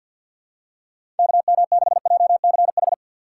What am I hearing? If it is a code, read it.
Morse code, heard as a keyed tone, about 52 wpm, 700 Hz. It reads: KM6JCH
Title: USA Call Signs - 6 Characters - 6x Character Spacing - 50wpm